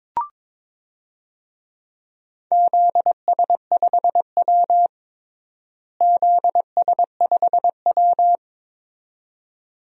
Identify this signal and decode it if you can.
Morse code, heard as a keyed tone, about 21 wpm, 700 Hz. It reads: ZS5W ZS5W
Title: DXCC-100 Call Signs - 4 Characters - Repeated 2x - 22wpm